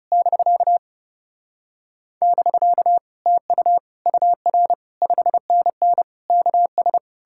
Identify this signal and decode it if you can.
Morse code, heard as a keyed tone, about 40 wpm, spaced as for 35 wpm, 700 Hz. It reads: <BK> <BK> TU UR 5NN KS
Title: Speed-Racing Common QSO Phrases - 30wpm